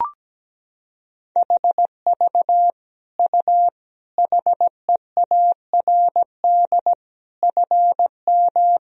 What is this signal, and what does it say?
Morse code, heard as a keyed tone, about 17 wpm, 700 Hz. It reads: HV U HEARD FM
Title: Sentences from Top 400 Words and Abbreviations 17wpm